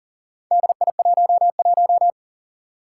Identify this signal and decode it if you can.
Morse code, heard as a keyed tone, about 40 wpm, 700 Hz. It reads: DI11
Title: Four Challenging Letter-Number Combinations 40wpm